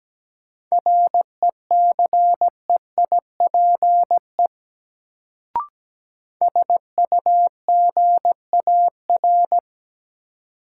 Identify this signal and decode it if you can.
Morse code, heard as a keyed tone, about 17 wpm, 700 Hz. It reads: RECEIPE SUGAR
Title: Sentences from Top 1200 Words - New Word Drill Alt - 17wpm